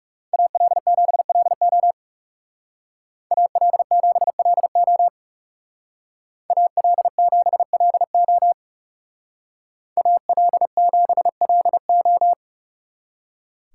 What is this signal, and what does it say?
Morse code, heard as a keyed tone, about 39 wpm, 700 Hz. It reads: AL7LO AL7LO AL7LO AL7LO